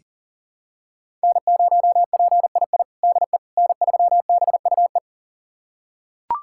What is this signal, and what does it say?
Morse code, heard as a keyed tone, about 40 wpm, 700 Hz. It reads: N0PII DE N3BUE
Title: USA Callsign DE Callsign - 5 Characters - 40wpm